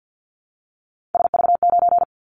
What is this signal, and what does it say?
Morse code, heard as a keyed tone, about 50 wpm, 700 Hz. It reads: SV9
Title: Callsign Prefixes for Top 100 DXCC - 50wpm